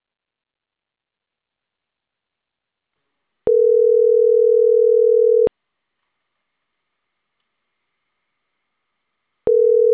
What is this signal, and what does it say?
Ringback tone (North America), 440 + 480 Hz.